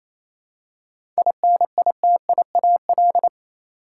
Morse code, heard as a keyed tone, about 28 wpm, 700 Hz. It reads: INITIAL
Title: Seven Letter Words - 28wpm